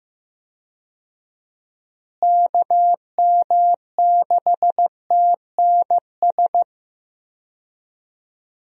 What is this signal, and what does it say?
Morse code, heard as a keyed tone, about 15 wpm, 700 Hz. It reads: KM6TNS
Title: USA Call Signs - 6 Characters - Repeated 2x - 15wpm